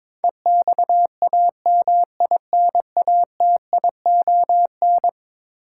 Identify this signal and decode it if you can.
Morse code, heard as a keyed tone, about 22 wpm, 700 Hz. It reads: EXAMINATION
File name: practice / 22wpm / Eleven Letter Words - 22wpm